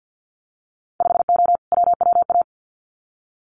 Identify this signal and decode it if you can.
Morse code, heard as a keyed tone, about 50 wpm, 700 Hz. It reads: 5O WWA
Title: Winter Field Day Exchanges Encore 50wpm